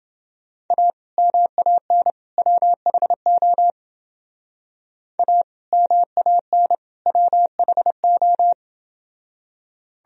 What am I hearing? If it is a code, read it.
Morse code, heard as a keyed tone, about 29 wpm, 700 Hz. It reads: A MAN WHO A MAN WHO